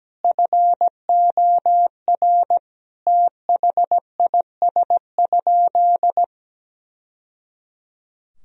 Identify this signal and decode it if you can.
Morse code, heard as a keyed tone, about 17 wpm, 700 Hz. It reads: FOR THIS?
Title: Sentences from Top 1000 Words - Review Part 4 - 17wpm